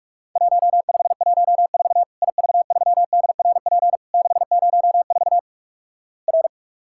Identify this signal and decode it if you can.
Morse code, heard as a keyed tone, about 47 wpm, 700 Hz. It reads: 1514 IV3DRP 604 R
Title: Worked All Europe DX Contest - QTC Exchange - 45wpm